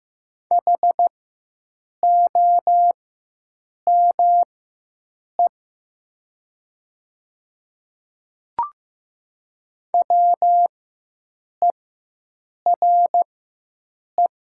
Morse code, heard as a keyed tone, about 15 wpm, spaced as for 7 wpm, 700 Hz. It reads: HOME WERE
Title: 4x Character Spacing - Top 200 Words - 15wpm